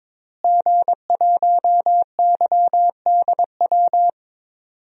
Morse code, heard as a keyed tone, about 22 wpm, 700 Hz. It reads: G1YDW
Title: DXCC-100 Call Signs - 5 Characters - Repeated 2x - 22wpm